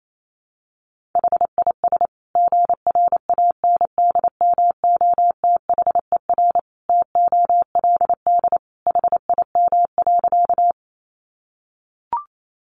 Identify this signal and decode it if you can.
Morse code, heard as a keyed tone, about 28 wpm, 700 Hz. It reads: HIS GRANDMOTHER TOLD HIM.
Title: Sentences from Top 1200 Words 28wpm